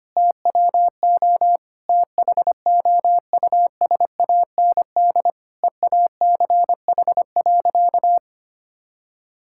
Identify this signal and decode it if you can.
Morse code, heard as a keyed tone, about 25 wpm, 700 Hz. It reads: TWO THOUSAND EACH.
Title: Sentences from Top 1000 Words - Review Part 5 - 25wpm